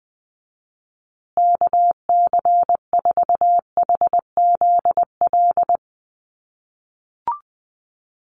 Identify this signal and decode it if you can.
Morse code, heard as a keyed tone, about 20 wpm, 700 Hz. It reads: KC4HZL
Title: USA Call Signs Repeated 3x 20wpm